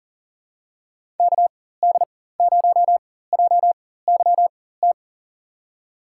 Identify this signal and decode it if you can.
Morse code, heard as a keyed tone, about 40 wpm, spaced as for 19 wpm, 700 Hz. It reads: KD0JYT
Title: USA Call Signs - 6 Characters - 4x Character Spacing - 40wpm